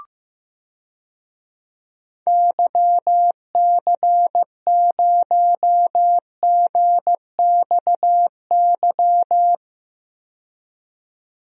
Morse code, heard as a keyed tone, about 15 wpm, 700 Hz. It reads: YC0GXY
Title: DXCC-100 Call Signs - 6 Characters - 15wpm